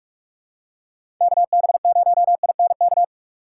Morse code, heard as a keyed tone, about 42 wpm, 700 Hz. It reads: KB0INK
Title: USA Call Signs - 6 Characters - 3x Character Spacing - 45wpm